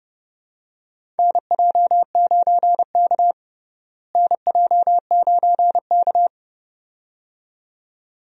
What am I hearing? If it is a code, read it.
Morse code, heard as a keyed tone, about 32 wpm, 700 Hz. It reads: NJ9K NJ9K
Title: USA Call Signs - 4 Characters - Repeated 2x - 30wpm